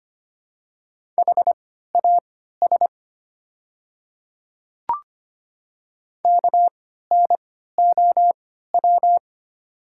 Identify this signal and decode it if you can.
Morse code, heard as a keyed tone, about 25 wpm, spaced as for 14 wpm, 700 Hz. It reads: HAS KNOW
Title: 3x Character Spacing - Top 100 Words - 25wpm